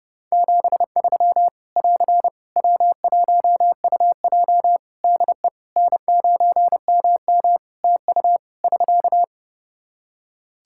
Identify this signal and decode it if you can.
Morse code, heard as a keyed tone, about 28 wpm, 700 Hz. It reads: 73 <AR> W1UJ DE N9MM TU <SK>
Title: Practice QSO - Exchange 4 - Ending - 30wpm